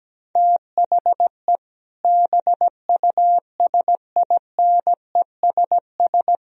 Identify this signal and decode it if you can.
Morse code, heard as a keyed tone, about 17 wpm, 700 Hz. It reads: THE BUSINESS
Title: Sentences from Top 1000 Words - Review Part 1 - 17wpm